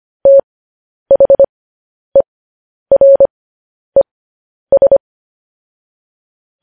Morse code, heard as a keyed tone, about 25 wpm, spaced as for 10 wpm, 550 Hz. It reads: THERES
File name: podcast / News headlines-1x-25x10